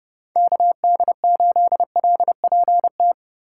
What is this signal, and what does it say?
Morse code, heard as a keyed tone, about 30 wpm, 700 Hz. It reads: KD8LPT
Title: USA Call Signs - 6 Characters - 3x Character Spacing - 30wpm